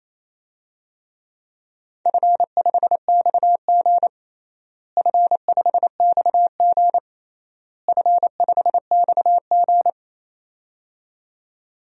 Morse code, heard as a keyed tone, about 28 wpm, 700 Hz. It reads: F5XG F5XG F5XG
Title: DXCC-100 Call Signs - 4 Characters - Repeated 3x - 28wpm